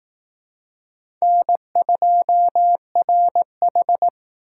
Morse code, heard as a keyed tone, about 20 wpm, 700 Hz. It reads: N2RH